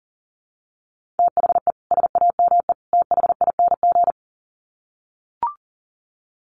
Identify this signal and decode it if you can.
Morse code, heard as a keyed tone, about 40 wpm, 700 Hz. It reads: THE SAME THING
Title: Sets of 3 Words 40wpm